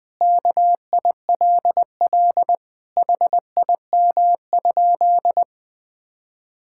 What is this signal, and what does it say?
Morse code, heard as a keyed tone, about 21 wpm, 700 Hz. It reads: KILL HIM?
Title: Sentences from Top 1600 Words 20wpm